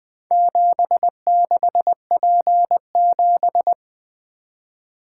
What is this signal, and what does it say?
Morse code, heard as a keyed tone, about 21 wpm, 700 Hz. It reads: 76P7
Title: Four Letter-Number Combinations - 20wpm